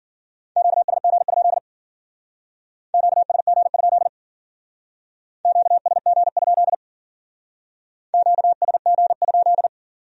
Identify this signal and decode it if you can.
Morse code, heard as a keyed tone, about 52 wpm, 700 Hz. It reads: QSG? QSG? QSG? QSG?